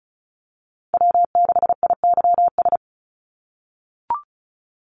Morse code, heard as a keyed tone, about 36 wpm, 700 Hz. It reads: W6IYS
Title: NA SOTA Chaser Callsigns 35wpm